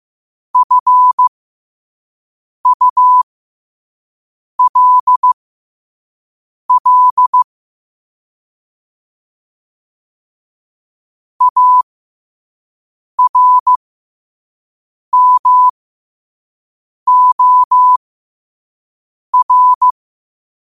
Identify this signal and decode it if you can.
Morse code, heard as a keyed tone, about 15 wpm, spaced as for 5 wpm, 1000 Hz. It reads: FULL ARMOR
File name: qotdfiles / 5wpm